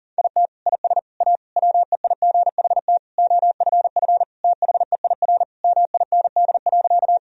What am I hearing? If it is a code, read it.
Morse code, heard as a keyed tone, about 40 wpm, 700 Hz. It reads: IT IS A WEIGHT OFF THEIR MIND.
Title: Sentences from Top 1000 Words - Review Part 3 - 40wpm